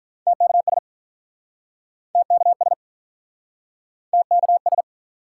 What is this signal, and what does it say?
Morse code, heard as a keyed tone, about 45 wpm, 700 Hz. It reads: TKS TKS TKS